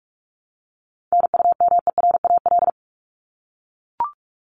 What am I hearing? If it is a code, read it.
Morse code, heard as a keyed tone, about 44 wpm, 700 Hz. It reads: NUMERAL
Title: Top 500 Words Repeated 45wpm